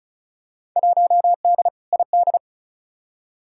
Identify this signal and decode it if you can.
Morse code, heard as a keyed tone, about 36 wpm, 700 Hz. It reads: 1D ID
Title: Field Day Exchanges - 35wpm